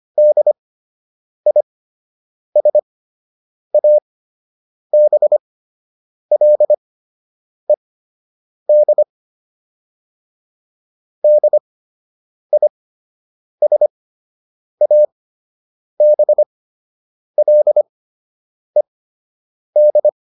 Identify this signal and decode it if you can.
Morse code, heard as a keyed tone, about 25 wpm, spaced as for 8 wpm, 600 Hz. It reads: DISABLED DISABLED